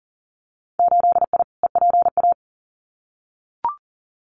Morse code, heard as a keyed tone, about 40 wpm, 700 Hz. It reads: 8I EPA
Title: Winter Field Day Exchanges Encore 40wpm